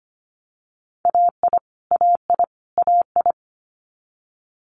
Morse code, heard as a keyed tone, about 25 wpm, 700 Hz. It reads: AI AI AI
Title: Multi-letter Phonemes Repeated 3x 25wpm